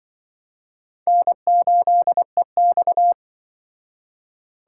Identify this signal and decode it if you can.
Morse code, heard as a keyed tone, about 23 wpm, 700 Hz. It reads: N8EX
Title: Speed Racing - USA 1x2 Call Signs 20wpm